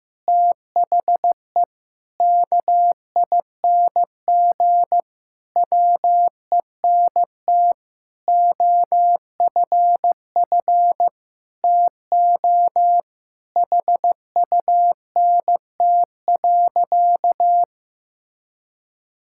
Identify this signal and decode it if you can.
Morse code, heard as a keyed tone, about 15 wpm, 700 Hz. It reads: THE KING WENT OFF TO HUNT.
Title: Sentences from Top 1600 Words 15wpm